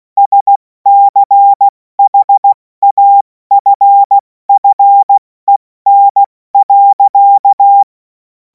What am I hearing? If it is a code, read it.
Morse code, heard as a keyed tone, about 16 wpm, 800 Hz. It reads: SCHAFFEN.
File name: Morsekurs / Witze / Witz 24. Februar 2026 70 F80